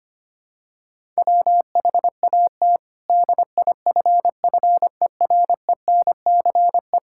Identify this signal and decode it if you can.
Morse code, heard as a keyed tone, about 25 wpm, 700 Hz. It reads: WHAT DIFFERENCE
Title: Sentences from Top 1000 Words - Review Part 5 - 25wpm